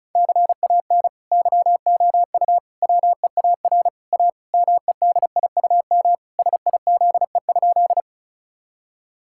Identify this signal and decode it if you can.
Morse code, heard as a keyed tone, about 34 wpm, 700 Hz. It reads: CAN YOU WEAR A MEDIUM SIZE?
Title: Sentences from Top 1000 Words - Review Part 6 - 35wpm